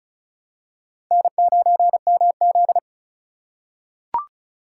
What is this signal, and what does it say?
Morse code, heard as a keyed tone, about 36 wpm, 700 Hz. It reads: N9MZ
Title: USA Call Signs - 4 Characters - 6x Character Spacing - 35wpm